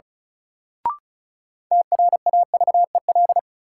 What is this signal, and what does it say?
Morse code, heard as a keyed tone, about 34 wpm, 700 Hz. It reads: TRAVEL
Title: Sentences from Top 400 Words - New Word Drill - 35wpm